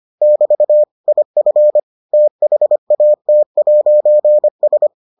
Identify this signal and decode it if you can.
Morse code, heard as a keyed tone, about 25 wpm, 600 Hz. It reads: <BT> IF THAT'S